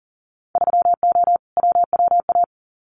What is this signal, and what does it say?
Morse code, heard as a keyed tone, about 40 wpm, 700 Hz. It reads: 3O WWA
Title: Winter Field Day Exchanges Encore 40wpm